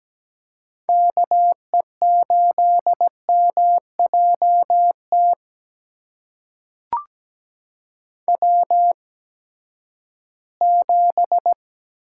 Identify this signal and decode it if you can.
Morse code, heard as a keyed tone, about 17 wpm, spaced as for 12 wpm, 700 Hz. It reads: KE8MJT W7
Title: USA Call Signs - 8x Character Spacing - 17wpm